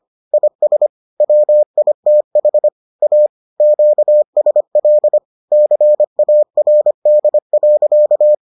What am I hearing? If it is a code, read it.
Morse code, heard as a keyed tone, about 25 wpm, 600 Hz. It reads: IS WITH A QSL CARD.